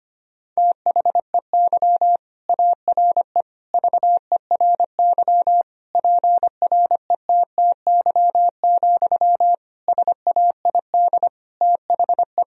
Morse code, heard as a keyed tone, about 25 wpm, 700 Hz. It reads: THEY ARE VERY PRETTY, SAID THE
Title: Sentences from Top 1000 Words - Review Part 4 - 25wpm